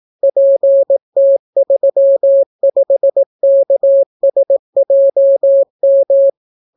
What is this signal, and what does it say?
Morse code, heard as a keyed tone, about 18 wpm, 550 Hz. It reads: PT35KSJM